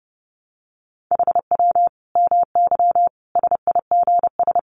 Morse code, heard as a keyed tone, about 30 wpm, 700 Hz. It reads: HW MY SIGS
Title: Common QSO Phrases 30wpm